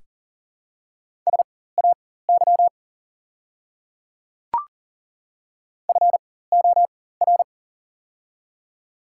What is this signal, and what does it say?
Morse code, heard as a keyed tone, about 40 wpm, spaced as for 19 wpm, 700 Hz. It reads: SAY FOR
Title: 4x Character Spacing - Top 300 Words - 40wpm